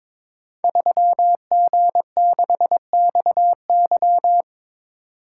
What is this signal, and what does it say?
Morse code, heard as a keyed tone, about 22 wpm, 690 Hz. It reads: 3G6XY
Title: Code Groups - 5 Letter:Numbers - 22wpm